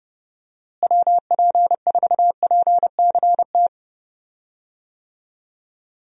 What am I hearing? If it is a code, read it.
Morse code, heard as a keyed tone, about 32 wpm, 700 Hz. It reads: WP4PCT
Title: USA Call Signs - 6 Characters - Repeated 3x - 30wpm